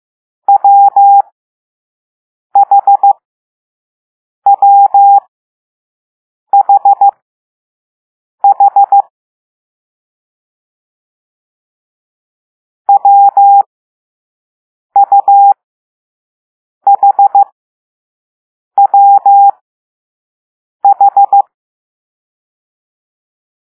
Morse code, heard as a keyed tone, about 15 wpm, spaced as for 5 wpm, 800 Hz. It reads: WHWHH WUHWH